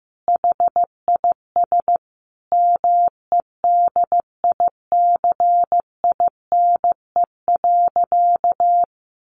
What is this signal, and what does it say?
Morse code, heard as a keyed tone, about 15 wpm, 700 Hz. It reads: HIS MEDICINE.
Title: Sentences from Top 900 Words 15wpm